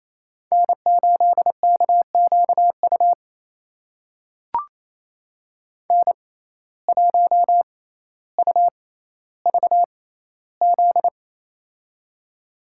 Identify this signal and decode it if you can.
Morse code, heard as a keyed tone, about 28 wpm, spaced as for 14 wpm, 700 Hz. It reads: N8KQU N1UVZ
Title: USA Call Signs - 5 Characters - 6x Character Spacing - 28wpm